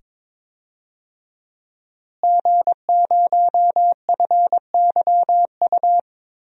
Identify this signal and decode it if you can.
Morse code, heard as a keyed tone, about 22 wpm, 700 Hz. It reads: G0FYU
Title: DXCC-100 Call Signs - 5 Characters - Repeated 2x - 22wpm